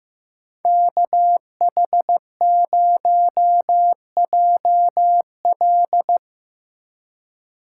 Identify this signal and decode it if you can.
Morse code, heard as a keyed tone, about 15 wpm, 700 Hz. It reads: KH0JL
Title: USA Call Signs - 5 Characters - 4x Character Spacing - 15wpm